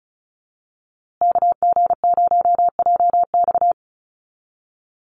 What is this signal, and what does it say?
Morse code, heard as a keyed tone, about 34 wpm, 700 Hz. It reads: KG0JX
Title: USA Call Signs Repeated 3x 35wpm